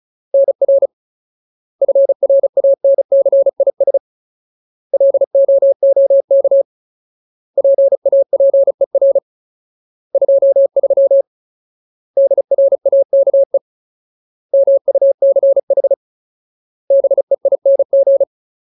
Morse code, heard as a keyed tone, about 34 wpm, 550 Hz. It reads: NR FRANCIS LOOK PAPER 23 DRAKE MUCH BEING
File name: Ultimate Word List / 35WPM Ultimate